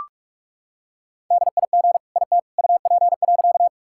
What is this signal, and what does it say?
Morse code, heard as a keyed tone, about 46 wpm, 700 Hz. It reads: DIG IT UP.
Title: Sentences from Top 1600 Words 45wpm